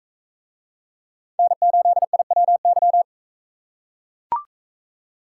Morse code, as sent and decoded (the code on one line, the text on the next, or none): -. ---.. .. .-- -.--
N8IWY